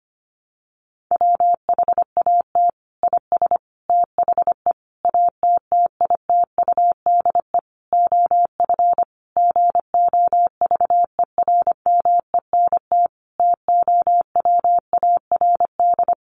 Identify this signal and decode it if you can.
Morse code, heard as a keyed tone, about 25 wpm, 700 Hz. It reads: WHAT IS THE ATTITUDE OF GOVERMENT TOWARD
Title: Sentences from Top 1300 Words 25wpm